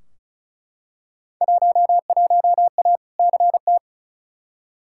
Morse code, heard as a keyed tone, about 36 wpm, 700 Hz. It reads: 11A CT
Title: Field Day Exchanges Encore - 35wpm